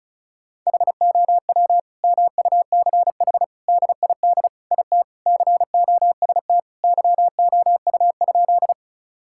Morse code, heard as a keyed tone, about 36 wpm, 700 Hz. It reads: HOW MUCH DID IT COST YOU?
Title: Sentences from Top 1000 Words - Review Part 5 - 35wpm